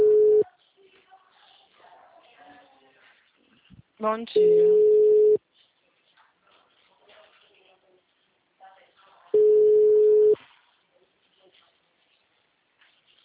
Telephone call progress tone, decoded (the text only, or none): ringback tone